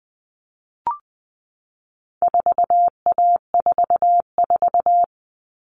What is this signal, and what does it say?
Morse code, heard as a keyed tone, about 20 wpm, 700 Hz. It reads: 4A44
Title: All Cut Numbers - 4-Digit 20wpm